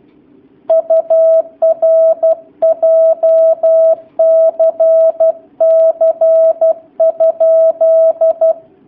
Morse code, heard as a keyed tone, about 12 wpm, 650 Hz. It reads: UR JCC?
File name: jh8cbh / 2019cw / sp11- w